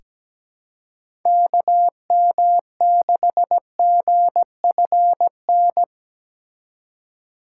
Morse code, heard as a keyed tone, about 17 wpm, 700 Hz. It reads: KM6GFN
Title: USA Call Signs - 6 Characters - 3x Character Spacing - 17wpm